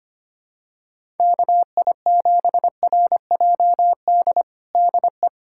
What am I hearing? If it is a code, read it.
Morse code, heard as a keyed tone, about 25 wpm, 700 Hz. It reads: KI7RJD DE
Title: USA Callsign DE Callsign - 6 Characters - 25wpm